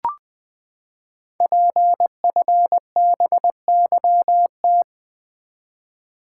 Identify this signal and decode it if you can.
Morse code, heard as a keyed tone, about 20 wpm, 700 Hz. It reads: PFBYT
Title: Code Groups - 5 Letters - 20wpm